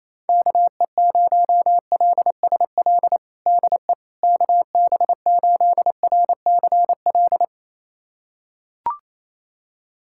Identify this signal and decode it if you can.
Morse code, heard as a keyed tone, about 28 wpm, 700 Hz. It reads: KE0LSL DE KB8RCL
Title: USA Callsign DE Callsign 28wpm